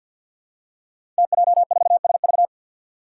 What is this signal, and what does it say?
Morse code, heard as a keyed tone, about 53 wpm, 700 Hz. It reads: TJ4SV
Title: Five Challenging Letter-Number Combinations 50wpm